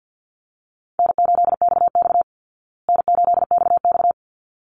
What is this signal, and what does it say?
Morse code, heard as a keyed tone, about 49 wpm, 700 Hz. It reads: N8XX N8XX
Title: Easy Call Signs Repeated 2x - 50wpm